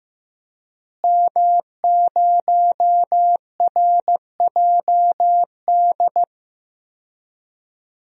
Morse code, heard as a keyed tone, about 15 wpm, 700 Hz. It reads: M0RJD
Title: DXCC-100 Call Signs - 5 Characters - Repeated 3x - 15wpm